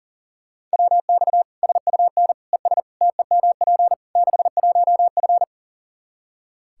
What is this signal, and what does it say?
Morse code, heard as a keyed tone, about 40 wpm, 700 Hz. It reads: WX SUN ES TEMP 61F
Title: Practice QSO - Exchange 2 - Info - 40wpm